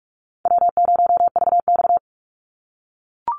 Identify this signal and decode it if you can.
Morse code, heard as a keyed tone, about 44 wpm, 700 Hz. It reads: W0VX
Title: Call Signs - 45wpm